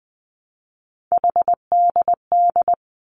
Morse code, heard as a keyed tone, about 20 wpm, 700 Hz. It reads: HDD
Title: Three Challenging Letter-Number Combinations 20wpm